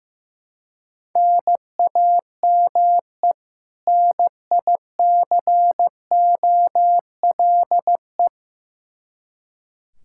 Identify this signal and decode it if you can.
Morse code, heard as a keyed tone, about 15 wpm, 700 Hz. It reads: NAME NICOLE
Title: Practice QSO - Exchange 1 - Beginning - 15wpm